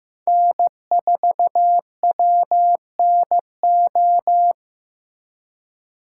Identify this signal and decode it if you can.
Morse code, heard as a keyed tone, about 17 wpm, 700 Hz. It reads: N4WNO
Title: Speed Racing - USA 1x3 Call Signs 15wpm